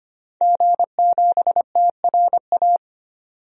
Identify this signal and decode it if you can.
Morse code, heard as a keyed tone, about 25 wpm, 700 Hz. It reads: G7TRA
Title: DXCC-100 Call Signs - 5 Characters - 25wpm